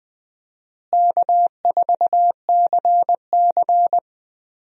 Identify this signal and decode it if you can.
Morse code, heard as a keyed tone, about 20 wpm, 700 Hz. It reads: K4CC
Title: USA Call Signs - 4 Characters - Repeated 3x - 20wpm